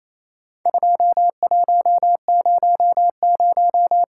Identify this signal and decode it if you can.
Morse code, heard as a keyed tone, about 28 wpm, 700 Hz. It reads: 2100
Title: Worked All Europe DX Contest - QTC Exchange - 28wpm